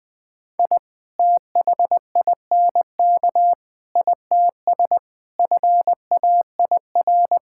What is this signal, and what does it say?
Morse code, heard as a keyed tone, about 20 wpm, 700 Hz. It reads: I THINK ITS FAIR
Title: Sentences with 7-Word Phrases - 20wpm